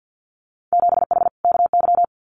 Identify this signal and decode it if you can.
Morse code, heard as a keyed tone, about 50 wpm, 700 Hz. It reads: 7H KY
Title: Winter Field Day Exchanges Encore 50wpm